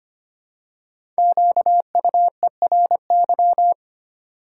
Morse code, heard as a keyed tone, about 25 wpm, 700 Hz. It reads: QUERY
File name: 5 Letter Words - First 3 Letters Given - 25wpm